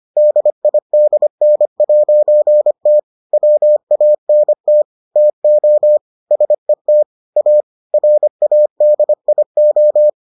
Morse code, heard as a keyed tone, about 25 wpm, 600 Hz. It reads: DIDN'T WANT TO SET A RADIO